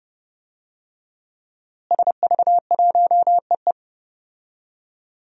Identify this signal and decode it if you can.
Morse code, heard as a keyed tone, about 29 wpm, 700 Hz. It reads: SV1EE
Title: DXCC-100 Call Signs - 5 Characters - Repeated 2x - 30wpm